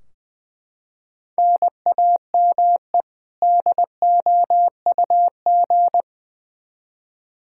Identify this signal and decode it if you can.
Morse code, heard as a keyed tone, about 20 wpm, 700 Hz. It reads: NAME DOUG
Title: Practice QSO - Exchange 1 - Beginning - 20wpm